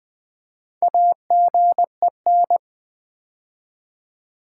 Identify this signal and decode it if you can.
Morse code, heard as a keyed tone, about 20 wpm, 700 Hz. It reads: AGEN
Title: Build A Word - 2 Letters At A Time - 6-Letter Words - 20wpm